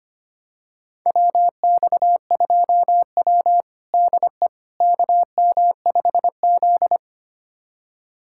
Morse code, heard as a keyed tone, about 25 wpm, 700 Hz. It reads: WX2W DE KM5Z
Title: USA Callsign DE Callsign - 4 Characters - 25wpm